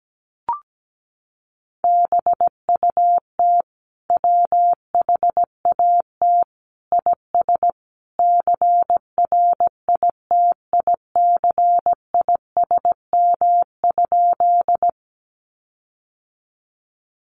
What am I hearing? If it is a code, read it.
Morse code, heard as a keyed tone, about 17 wpm, 700 Hz. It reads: BUT WHAT IS CRITICISM?
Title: Sentences from Top 1300 Words 17wpm